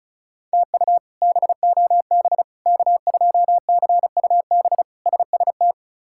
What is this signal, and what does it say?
Morse code, heard as a keyed tone, about 36 wpm, 700 Hz. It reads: TU BOB K2CUB SST
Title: K1USN SST QSO Practice - 35wpm